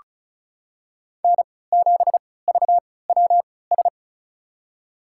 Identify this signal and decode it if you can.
Morse code, heard as a keyed tone, about 36 wpm, spaced as for 20 wpm, 700 Hz. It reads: N7VWS
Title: USA Call Signs - 5 Characters - 3x Character Spacing - 35wpm